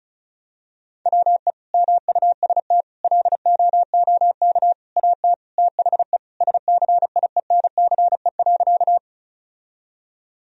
Morse code, heard as a keyed tone, about 36 wpm, 700 Hz. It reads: WE MUST LOOK AT THE SCIENCE.
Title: Sentences from Top 1000 Words - Review Part 2 - 35wpm